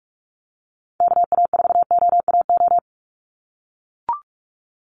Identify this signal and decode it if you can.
Morse code, heard as a keyed tone, about 44 wpm, 700 Hz. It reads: KA4OAO
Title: USA Call Signs 45wpm